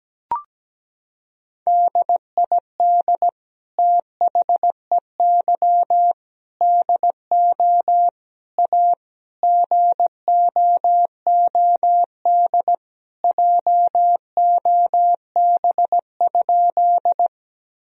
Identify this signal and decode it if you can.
Morse code, heard as a keyed tone, about 17 wpm, 700 Hz. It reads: DID THEY DO A GOOD JOB?
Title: Sentences from Top 1000 Words - Review Part 3 - 17wpm